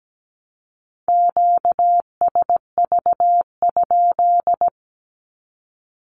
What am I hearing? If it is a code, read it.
Morse code, heard as a keyed tone, about 17 wpm, 700 Hz. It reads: QSV?
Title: Q-Codes Repeated 3x 17wpm